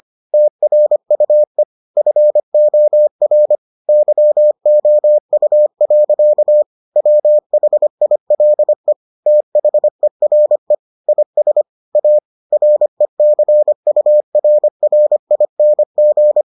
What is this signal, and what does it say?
Morse code, heard as a keyed tone, about 25 wpm, 600 Hz. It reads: TRUE FOR YOU. WHILE THERE IS A RECURRING